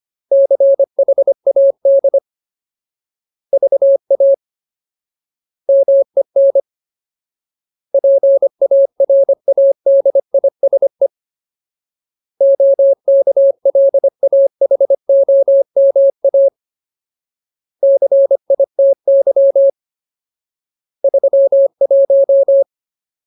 Morse code, heard as a keyed tone, about 25 wpm, 550 Hz. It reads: CHAD VA MEN PARADISE OKLAHOMA CITY 31
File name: Ultimate Word List / 25WPM Ultimate